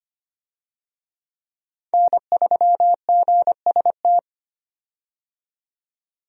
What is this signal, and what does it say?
Morse code, heard as a keyed tone, about 25 wpm, 700 Hz. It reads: N3GST
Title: USA Call Signs - 5 Characters - 25wpm